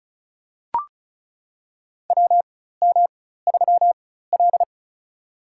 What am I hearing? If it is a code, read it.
Morse code, heard as a keyed tone, about 36 wpm, spaced as for 16 wpm, 700 Hz. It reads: WM3L
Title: USA Call Signs - 4 Characters - 4x Character Spacing - 35wpm